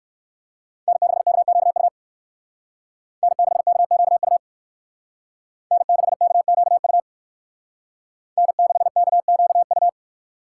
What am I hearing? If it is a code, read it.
Morse code, heard as a keyed tone, about 54 wpm, 700 Hz. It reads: N6KQU N6KQU N6KQU N6KQU
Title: Speed Racing - USA Call Signs 45wpm